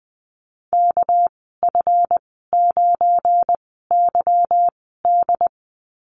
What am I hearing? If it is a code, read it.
Morse code, heard as a keyed tone, about 20 wpm, spaced as for 14 wpm, 700 Hz. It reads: KF9YD
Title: 245 Lesson 39g - Letter z - Call Signs 20wpm